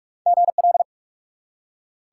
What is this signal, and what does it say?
Morse code, heard as a keyed tone, about 46 wpm, 700 Hz. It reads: GL
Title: Top 90 QSO Elements 45wpm